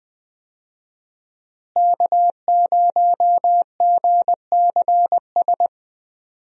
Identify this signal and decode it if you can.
Morse code, heard as a keyed tone, about 21 wpm, 700 Hz. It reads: K0GCS
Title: USA Call Signs - 5 Characters - Repeated 2x - 20wpm